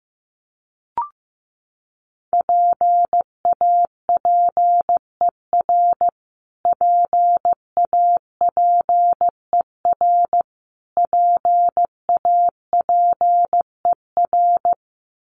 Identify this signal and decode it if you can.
Morse code, heard as a keyed tone, about 15 wpm, 700 Hz. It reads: PAPER PAPER PAPER
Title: Top 300 Words Repeated 15wpm